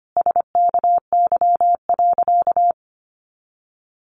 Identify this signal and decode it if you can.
Morse code, heard as a keyed tone, about 25 wpm, 700 Hz. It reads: SKY.
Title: A Sound Of Thunder 25wpm